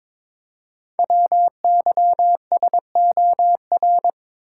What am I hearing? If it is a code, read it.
Morse code, heard as a keyed tone, about 22 wpm, 700 Hz. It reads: WYSOR
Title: Code Groups - 5 Letters - 22wpm